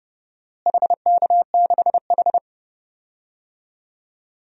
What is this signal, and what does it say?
Morse code, heard as a keyed tone, about 32 wpm, 700 Hz. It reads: HK6H
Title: DXCC-100 Call Signs - 4 Characters - Repeated 2x - 30wpm